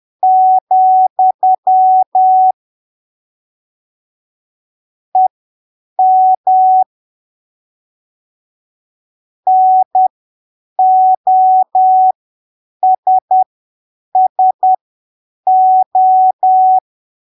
Morse code, heard as a keyed tone, about 10 wpm, spaced as for 7 wpm, 750 Hz. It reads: , EM NOSSO
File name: arquivos / texto / texto 22 10ppm